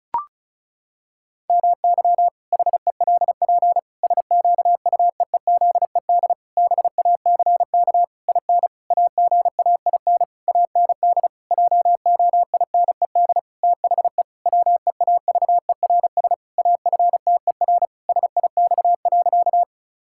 Morse code, heard as a keyed tone, about 36 wpm, 700 Hz. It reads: MY HELP SQUEEZED BACK IN AGAIN AND JOINED THE WEAVERS AFTER SIX.
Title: Pangram Challenge v2 - 35wpm